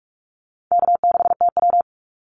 Morse code, heard as a keyed tone, about 44 wpm, 700 Hz. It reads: K6TW
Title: NA SOTA Activator Callsigns 45wpm